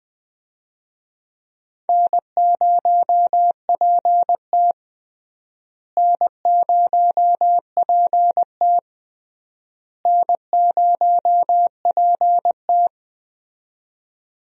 Morse code, heard as a keyed tone, about 21 wpm, 700 Hz. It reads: N0PT N0PT N0PT
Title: USA Call Signs - 4 Characters - Repeated 3x - 20wpm